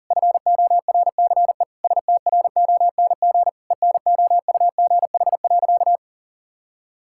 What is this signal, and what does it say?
Morse code, heard as a keyed tone, about 40 wpm, 700 Hz. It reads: FORCE STRONG ENOUGH.
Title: Sentences from Top 1600 Words 40wpm